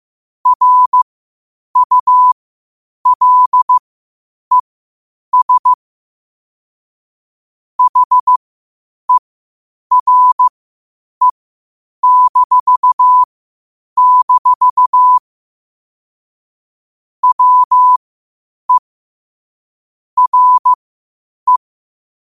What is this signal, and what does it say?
Morse code, heard as a keyed tone, about 15 wpm, spaced as for 8 wpm, 1000 Hz. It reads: RULES HERE-- WE RE
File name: qotdfiles / 8wpm